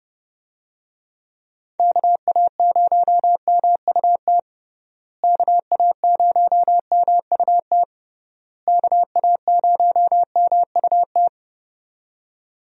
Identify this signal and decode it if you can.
Morse code, heard as a keyed tone, about 30 wpm, 700 Hz. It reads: KA0MUT KA0MUT KA0MUT
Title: USA Call Signs - 6 Characters - Repeated 3x - 30wpm